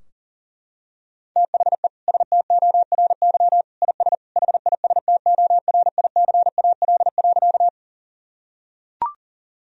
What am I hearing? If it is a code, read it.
Morse code, heard as a keyed tone, about 40 wpm, 700 Hz. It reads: THE STORY IS HISTORICAL.
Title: Sentences from Top 1000 Words - Review Part 4 - 40wpm